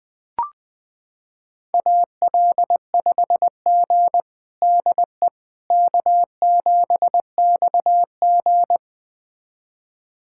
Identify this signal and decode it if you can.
Morse code, heard as a keyed tone, about 20 wpm, 700 Hz. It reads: AL5G DE K7XG
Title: USA Callsign DE Callsign - 4 Characters - 20wpm